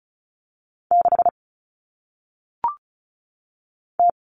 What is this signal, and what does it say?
Morse code, heard as a keyed tone, about 36 wpm, 700 Hz. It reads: 6 T
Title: Common Cut Numbers - 1-Digit 35wpm